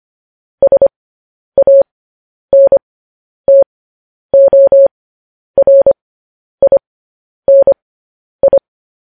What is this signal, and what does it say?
Morse code, heard as a keyed tone, about 25 wpm, spaced as for 10 wpm, 550 Hz. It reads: SANTORINI
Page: News Headlines in Morse Code at 10 WPM